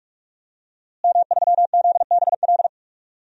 Morse code, heard as a keyed tone, about 43 wpm, 700 Hz. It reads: M3ZBL
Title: DXCC-100 Call Signs - 5 Characters - Repeated 2x - 45wpm